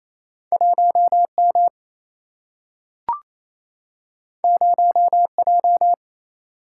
Morse code, heard as a keyed tone, about 28 wpm, 700 Hz. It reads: 1M 0J
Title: Single-Letter Single-Number 28wpm